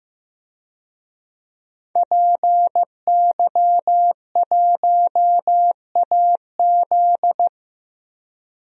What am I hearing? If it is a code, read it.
Morse code, heard as a keyed tone, about 15 wpm, 700 Hz. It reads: PY1AZ
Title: DXCC-100 Call Signs - 5 Characters - 15wpm